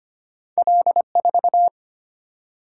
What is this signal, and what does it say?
Morse code, heard as a keyed tone, about 25 wpm, 700 Hz. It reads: L4
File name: Two Letter-Number Combinations - 25wpm